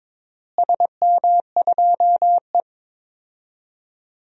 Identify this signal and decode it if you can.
Morse code, heard as a keyed tone, about 21 wpm, 700 Hz. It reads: SM2E
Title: DXCC-100 Call Signs - 4 Characters - Repeated 2x - 22wpm